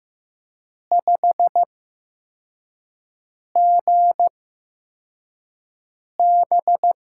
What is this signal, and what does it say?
Morse code, heard as a keyed tone, about 15 wpm, spaced as for 4 wpm, 700 Hz. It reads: 5GB
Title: USA Call Signs - 6 Characters - 8x Character Spacing - 15wpm